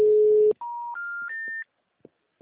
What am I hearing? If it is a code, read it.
Special information tone.
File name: out- -2013- - - .18088